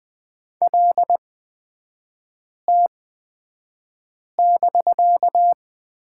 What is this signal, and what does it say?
Morse code, heard as a keyed tone, about 20 wpm, spaced as for 5 wpm, 700 Hz. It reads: LT<BK>
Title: 250 Lesson 40e - BK - Three Characters 20wpm